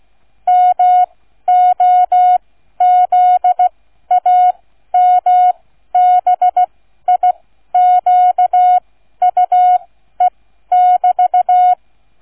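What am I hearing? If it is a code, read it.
Morse code, heard as a keyed tone, about 16 wpm, spaced as for 12 wpm, 730 Hz. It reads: MOZAMBIQUE<BT>
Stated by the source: Morse code receive practice, flight number, from ZL3TK.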